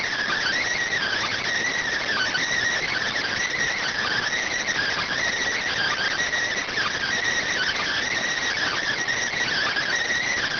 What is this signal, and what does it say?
Emergency Alert System (SAME) tones (header bursts), heard as harsh buzzing bursts. It is heard from 0.9 to 9.9 s.